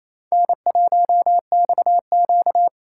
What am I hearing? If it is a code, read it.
Morse code, heard as a keyed tone, about 27 wpm, 700 Hz. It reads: N1XQ
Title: USA Call Signs - 4 Characters - 4x Character Spacing - 28wpm